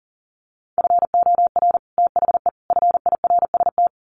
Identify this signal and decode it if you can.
Morse code, heard as a keyed tone, about 40 wpm, 700 Hz. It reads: FOR THE FIRST
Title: Sets of 3 Words 40wpm